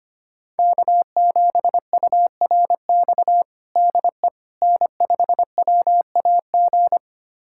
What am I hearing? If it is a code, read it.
Morse code, heard as a keyed tone, about 25 wpm, 700 Hz. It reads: K7URX DE N5WAG
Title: USA Callsign DE Callsign - 5 Characters - 25wpm